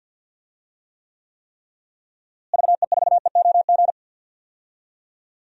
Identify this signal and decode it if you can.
Morse code, heard as a keyed tone, about 52 wpm, 700 Hz. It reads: VE4EOG
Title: DXCC-100 Call Signs - 6 Characters - Repeated 3x - 50wpm